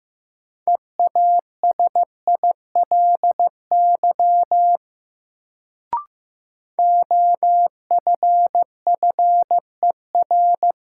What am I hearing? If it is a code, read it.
Morse code, heard as a keyed tone, about 15 wpm, 700 Hz. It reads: EASILY OFFER
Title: Sentences from Top 900 Words - New Word Drill - 15wpm